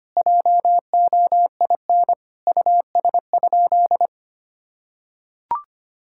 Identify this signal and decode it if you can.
Morse code, heard as a keyed tone, about 25 wpm, 700 Hz. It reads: JOIN US?
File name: practice / 25wpm / Sentences with 4-Word Phrases - 25wpm